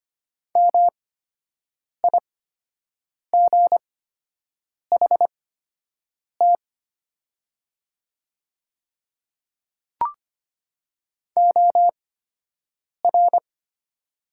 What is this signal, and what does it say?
Morse code, heard as a keyed tone, about 25 wpm, spaced as for 7 wpm, 700 Hz. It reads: MIGHT OR
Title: 8x Character Spacing - Top 300 Words - 25wpm